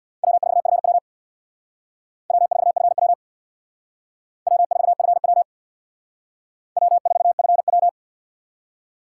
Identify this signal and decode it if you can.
Morse code, heard as a keyed tone, about 61 wpm, 700 Hz. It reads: W4FW W4FW W4FW W4FW
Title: Speed Racing - USA 1x2 Call Signs 50wpm